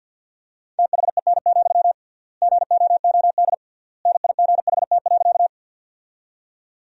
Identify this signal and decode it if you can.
Morse code, heard as a keyed tone, about 50 wpm, 700 Hz. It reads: THEN, GOOD NIGHT.
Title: 3x Word-Spacing - Sentences from Top 300 Words - 50wpm